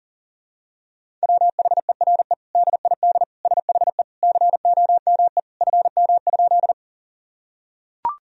Morse code, heard as a keyed tone, about 40 wpm, 700 Hz. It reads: WHERE DID SHE COME FM?
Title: Sentences from Top 200 Words and Abbreviations 40wpm